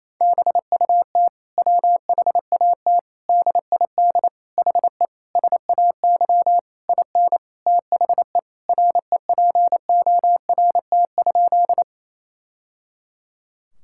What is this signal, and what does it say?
Morse code, heard as a keyed tone, about 28 wpm, 700 Hz. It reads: BUT WHAT DID HE SAY IN THE REPORT?
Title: Sentences from Top 1000 Words - Review Part 6 - 28wpm